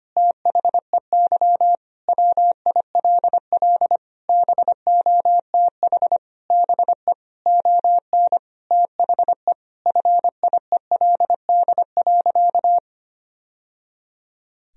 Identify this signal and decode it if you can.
Morse code, heard as a keyed tone, about 25 wpm, 700 Hz. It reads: THEY WILL BOTH BE ON THE FIELD.
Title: Sentences from Top 1000 Words - Review Part 2 - 25wpm